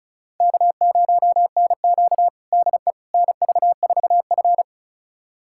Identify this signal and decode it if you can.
Morse code, heard as a keyed tone, about 36 wpm, 700 Hz. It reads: K0NQ DE NV4F
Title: USA Callsign DE Callsign - 4 Characters - 35wpm